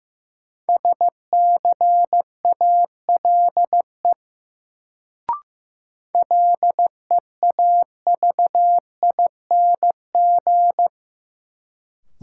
Morse code, heard as a keyed tone, about 15 wpm, 700 Hz. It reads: SCALE LEAVING
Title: Sentences from Top 1000 Words - Review Part 4 - New Word Drill - 15wpm